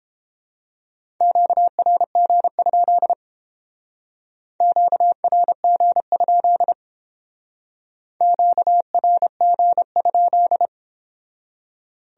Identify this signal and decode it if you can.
Morse code, heard as a keyed tone, about 29 wpm, 700 Hz. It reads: QRG? QRG? QRG?